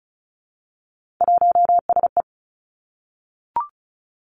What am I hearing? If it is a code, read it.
Morse code, heard as a keyed tone, about 36 wpm, 700 Hz. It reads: 1SE
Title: Three Challenging Letter-Number Combinations 35wpm